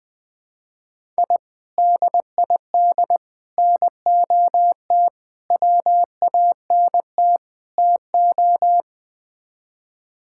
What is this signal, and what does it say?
Morse code, heard as a keyed tone, about 20 wpm, 700 Hz. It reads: I DID NOT WANT TO
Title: Build A Phrase - 5-Word Phrases - 20wpm